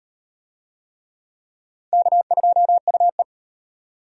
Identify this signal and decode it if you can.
Morse code, heard as a keyed tone, about 34 wpm, 700 Hz. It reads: K2UE